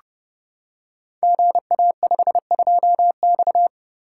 Morse code, heard as a keyed tone, about 30 wpm, 700 Hz. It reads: GA52X
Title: Code Groups - 5 Letter:Numbers - 30wpm